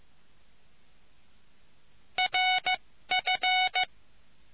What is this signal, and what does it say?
Morse code, heard as a keyed tone, about 15 wpm, spaced as for 13 wpm, 740 Hz. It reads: RF